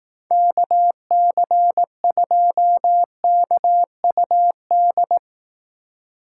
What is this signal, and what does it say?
Morse code, heard as a keyed tone, about 18 wpm, 700 Hz. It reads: KC2KUD
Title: Speed Racing - USA 2x3 Call Signs 15wpm